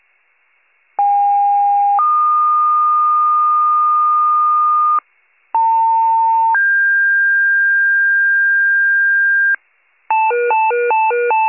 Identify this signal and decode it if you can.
Two-tone page, heard as a one-second tone, then a three-second tone, about 810 then 1220 Hz, about 900 then 1600 Hz, with noise. It is heard from 1.0 to 9.6 s.